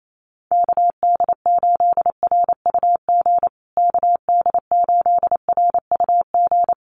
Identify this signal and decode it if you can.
Morse code, heard as a keyed tone, about 28 wpm, 700 Hz. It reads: KD8RUG KD8RUG
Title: USA Call Signs Repeated 2x 28wpm